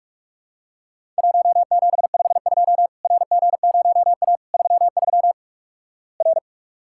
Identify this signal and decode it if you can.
Morse code, heard as a keyed tone, about 46 wpm, 700 Hz. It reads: 1752 RG0A 33 R
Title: Worked All Europe DX Contest - QTC Exchange - 45wpm